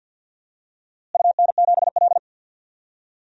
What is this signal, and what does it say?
Morse code, heard as a keyed tone, about 52 wpm, 700 Hz. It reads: UN7L
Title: DXCC-100 Call Signs - 4 Characters - Repeated 2x - 50wpm